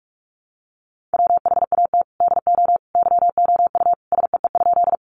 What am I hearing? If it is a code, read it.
Morse code, heard as a keyed tone, about 44 wpm, 700 Hz. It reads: WHAT DO YOU SEE?
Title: Sentences from Top 100 Words 45wpm